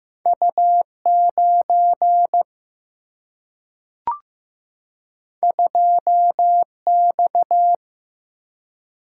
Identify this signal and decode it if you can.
Morse code, heard as a keyed tone, about 15 wpm, 700 Hz. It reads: U9 2X
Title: Single-Letter Single-Number 15wpm